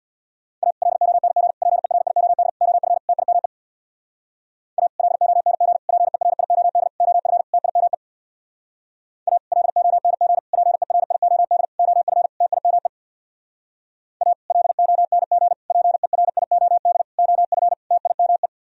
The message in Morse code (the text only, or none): A LONG PERIOD OF TIME A LONG PERIOD OF TIME A LONG PERIOD OF TIME A LONG PERIOD OF TIME